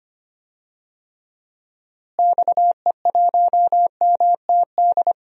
Morse code, heard as a keyed tone, about 25 wpm, 700 Hz. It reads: XE1MTD
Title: DXCC-100 Call Signs - 6 Characters - Repeated 2x - 25wpm